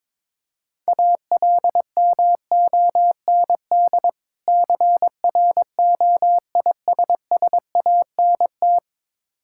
Morse code, heard as a keyed tone, about 22 wpm, 690 Hz. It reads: ALMOND CROISSANT
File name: Difficult Words with Hints - 22wpm